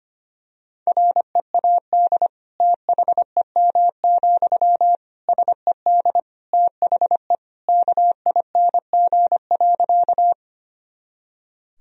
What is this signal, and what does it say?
Morse code, heard as a keyed tone, about 25 wpm, 700 Hz. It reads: READ THEM, SED THE KING.
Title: Sentences from Top 400 Words and Abbreviations 25wpm